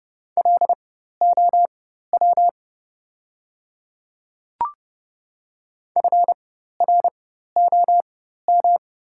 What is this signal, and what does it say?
Morse code, heard as a keyed tone, about 30 wpm, spaced as for 14 wpm, 700 Hz. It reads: LOW FROM
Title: 4x Character Spacing - Top 300 Words - 30wpm